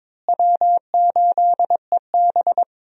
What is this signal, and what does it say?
Morse code, heard as a keyed tone, about 22 wpm, 700 Hz. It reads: W8EB
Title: USA Call Signs - 4 Characters - Repeated 3x - 22wpm